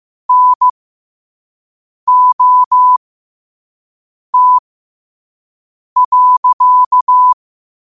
Morse code, heard as a keyed tone, about 15 wpm, spaced as for 5 wpm, 1000 Hz. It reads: NOT.
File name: qotdfiles / 5wpm